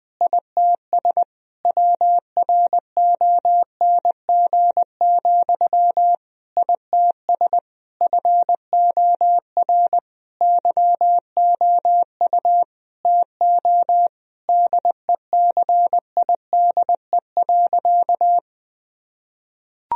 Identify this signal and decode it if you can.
Morse code, heard as a keyed tone, about 20 wpm, 700 Hz. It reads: ITS WRONG, ITS FOR YOU TO DECIDE.
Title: Sentences from Top 1000 Words - Review Part 3 - 20wpm